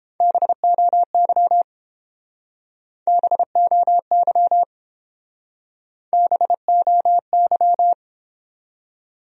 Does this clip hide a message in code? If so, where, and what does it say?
Yes: Morse code from 0.2 to 7.9 s, heard as a keyed tone, about 29 wpm, 700 Hz. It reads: BOY BOY BOY